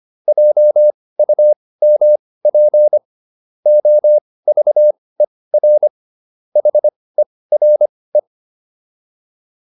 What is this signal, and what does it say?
Morse code, heard as a keyed tone, about 25 wpm, spaced as for 18 wpm, 600 Hz. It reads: JUMP OVER HERE